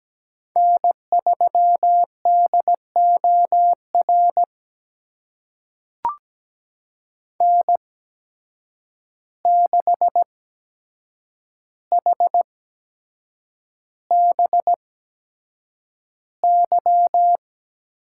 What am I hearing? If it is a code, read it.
Morse code, heard as a keyed tone, about 17 wpm, spaced as for 7 wpm, 700 Hz. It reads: N3DOR N6HBY
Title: USA Call Signs - 5 Characters - 8x Character Spacing - 17wpm